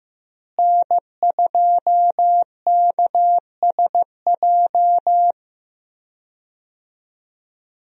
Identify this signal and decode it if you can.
Morse code, heard as a keyed tone, about 15 wpm, 700 Hz. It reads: N2KSJ
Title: USA Call Signs - 5 Characters - Repeated 3x - 15wpm